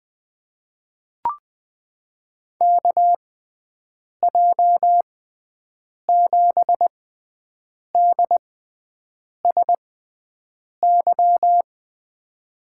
Morse code, heard as a keyed tone, about 20 wpm, spaced as for 7 wpm, 700 Hz. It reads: KJ7DSY
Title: USA Call Signs - 6x Character Spacing - 20wpm